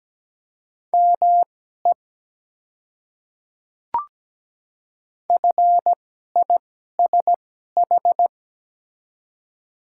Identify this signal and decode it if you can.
Morse code, heard as a keyed tone, about 17 wpm, spaced as for 12 wpm, 700 Hz. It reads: ME FISH
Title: 2x Character Spacing - Top 300 Words - 17wpm